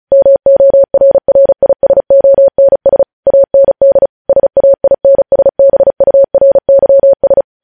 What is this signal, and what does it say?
Morse code, heard as a keyed tone, about 36 wpm, 550 Hz. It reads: MORRISONS AND SAINSBURYS
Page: News headlines in Morse code at 35 wpm